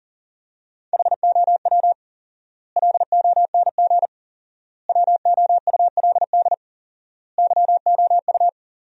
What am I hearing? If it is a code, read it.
Morse code, heard as a keyed tone, about 40 wpm, 700 Hz. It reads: HOW LONG WOULD YOU
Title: 4x Word-Spacing - Sentences from Top 500 Words - 40wpm